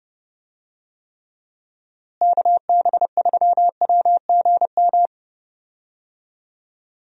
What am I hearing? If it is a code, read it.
Morse code, heard as a keyed tone, about 31 wpm, 700 Hz. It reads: KB3WGM
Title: USA Call Signs - 6 Characters - Repeated 3x - 30wpm